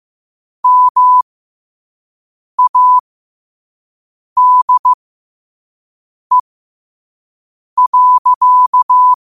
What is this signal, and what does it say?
Morse code, heard as a keyed tone, about 15 wpm, spaced as for 5 wpm, 1000 Hz. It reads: MADE.
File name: qotdfiles / 5wpm